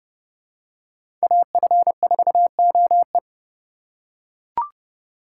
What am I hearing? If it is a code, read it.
Morse code, heard as a keyed tone, about 32 wpm, 700 Hz. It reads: AF4OE
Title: USA Call Signs - 4x Character Spacing - 30wpm